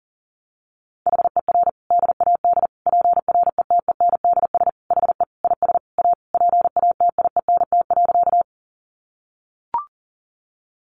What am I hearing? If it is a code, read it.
Morse code, heard as a keyed tone, about 37 wpm, 700 Hz. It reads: HER DAD PRETENDS HE IS A PATIENT.
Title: Additional Easy Phrases 40wpm